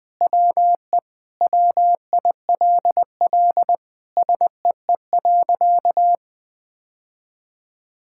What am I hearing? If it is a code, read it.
Morse code, heard as a keyed tone, about 20 wpm, 700 Hz. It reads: WE WILL SEE.
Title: Sentences with 3-Word Phrases - 20wpm